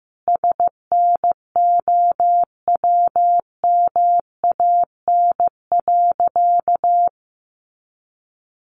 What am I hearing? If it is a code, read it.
Morse code, heard as a keyed tone, about 15 wpm, 700 Hz. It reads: SNOWMAN.
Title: Easy Phrases 15wpm